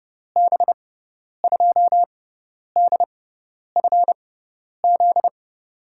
Morse code, heard as a keyed tone, about 31 wpm, spaced as for 10 wpm, 700 Hz. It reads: B2DFZ
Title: USA Call Signs - 6 Characters - 6x Character Spacing - 30wpm